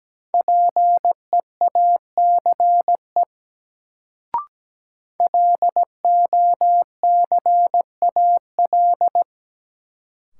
Morse code, heard as a keyed tone, about 17 wpm, 700 Hz. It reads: PEACE LOCAL
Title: Sentences from Top 1500 Words - New Word Drill Alt - 17wpm